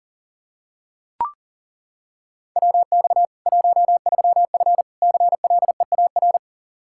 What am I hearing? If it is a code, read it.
Morse code, heard as a keyed tone, about 40 wpm, 700 Hz. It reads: WX 13F CLEAR
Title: K1USN SST QSO Practice 40wpm